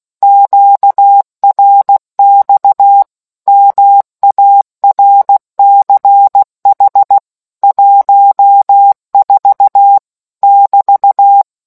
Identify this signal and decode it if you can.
Morse code, heard as a keyed tone, about 16 wpm, 800 Hz. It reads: QRX MARCH 14 <BT>